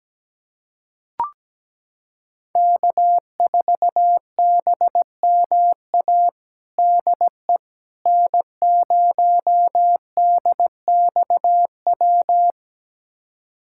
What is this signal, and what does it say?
Morse code, heard as a keyed tone, about 17 wpm, 700 Hz. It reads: K4BMA DE N0DXW
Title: USA Callsign DE Callsign - 5 Characters - 17wpm